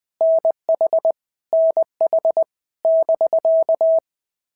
Morse code, heard as a keyed tone, about 20 wpm, 650 Hz. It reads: NH NH <BK>
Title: POTA QSO Practice - 20wpm